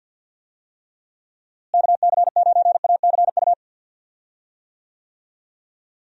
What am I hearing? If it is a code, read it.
Morse code, heard as a keyed tone, about 46 wpm, 700 Hz. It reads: KC9ACU
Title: USA Call Signs - 6 Characters - Repeated 3x - 50wpm